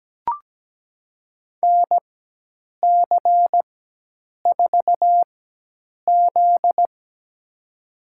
Morse code, heard as a keyed tone, about 17 wpm, spaced as for 8 wpm, 700 Hz. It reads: NC4Z
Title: USA Call Signs - 4 Characters - 4x Character Spacing - 17wpm